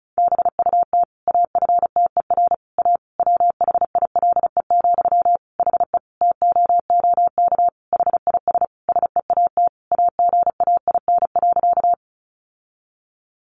Morse code, heard as a keyed tone, about 34 wpm, 700 Hz. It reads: BUT AFTER A WHILE, HE TOOK HIS SEAT AGAIN.
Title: Sentences from Top 1500 Words 35wpm